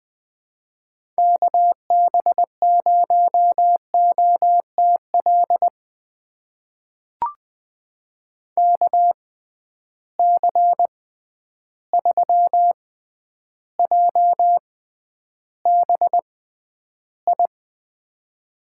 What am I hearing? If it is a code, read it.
Morse code, heard as a keyed tone, about 20 wpm, spaced as for 10 wpm, 700 Hz. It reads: KB0OTL KC3JBI
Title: USA Call Signs - 6 Characters - 6x Character Spacing - 20wpm